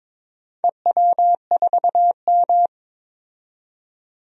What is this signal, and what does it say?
Morse code, heard as a keyed tone, about 21 wpm, 700 Hz. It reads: EW4M
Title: DXCC-100 Call Signs - 4 Characters - 22wpm